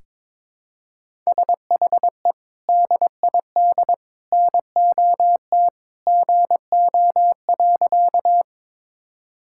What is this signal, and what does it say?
Morse code, heard as a keyed tone, about 22 wpm, 700 Hz. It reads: SHE DID NOT GO.
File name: practice / 22wpm / Sentences with 3-Word Phrases - 22wpm